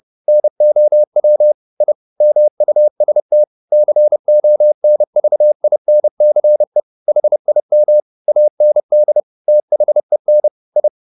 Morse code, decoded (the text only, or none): NOW I MUST CONVINCE HIM AND THEN I